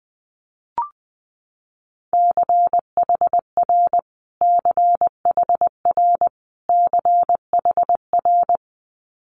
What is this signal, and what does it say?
Morse code, heard as a keyed tone, about 19 wpm, 700 Hz. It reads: CHR CHR CHR
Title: Multi-letter Phonemes Repeated 3x 20wpm